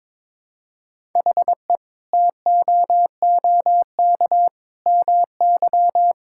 Morse code, heard as a keyed tone, about 22 wpm, 700 Hz. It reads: HE TOOK MY
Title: Sentences from Top 1000 Words - Review Part 6 - 22wpm